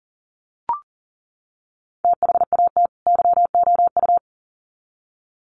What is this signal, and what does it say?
Morse code, heard as a keyed tone, about 40 wpm, 700 Hz. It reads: THAT YOU
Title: Sets of 2 Words 40wpm